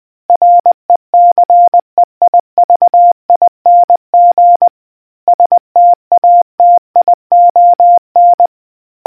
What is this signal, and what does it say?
Morse code, heard as a keyed tone, about 20 wpm, 700 Hz. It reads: RECEIVING STATION